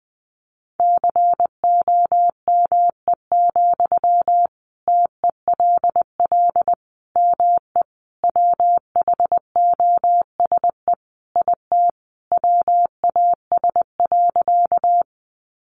Morse code, heard as a keyed tone, about 20 wpm, 700 Hz. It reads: COME, TELL ME WHOSE IT WAS.
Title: Sentences from Top 1200 Words 20wpm